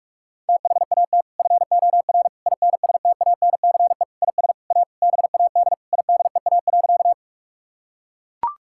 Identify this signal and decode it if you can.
Morse code, heard as a keyed tone, about 41 wpm, 700 Hz. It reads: THAT FOR INSTANCE IS A BAD IDEA.
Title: Sentences from Top 1000 Words - Review Part 5 - 45wpm